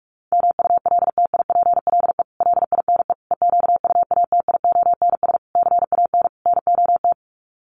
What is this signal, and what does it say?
Morse code, heard as a keyed tone, about 44 wpm, 700 Hz. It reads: MULTIPLE LINE EQUATIONS CAN NOT
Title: Sets of 5 Words 45wpm